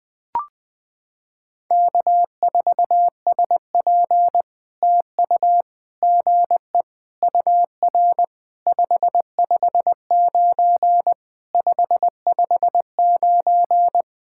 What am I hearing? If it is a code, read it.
Morse code, heard as a keyed tone, about 20 wpm, 700 Hz. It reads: K4SP TU GE UR 559 559
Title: POTA QSO Practice - 20wpm